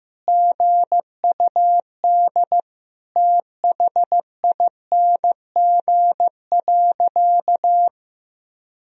Morse code, heard as a keyed tone, about 16 wpm, 700 Hz. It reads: GUD THING.
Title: Sentences from Top 300 Words and Abbreviations 15wpm